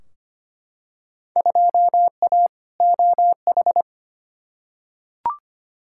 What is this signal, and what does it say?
Morse code, heard as a keyed tone, about 25 wpm, 700 Hz. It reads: 2A OH
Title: Field Day Exchanges - 25wpm